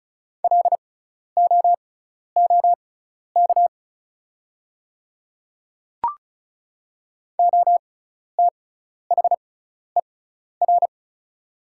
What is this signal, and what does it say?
Morse code, heard as a keyed tone, about 36 wpm, spaced as for 12 wpm, 690 Hz. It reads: LOOK OTHER
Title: 6x Character Spacing - Top 300 Words - 35wpm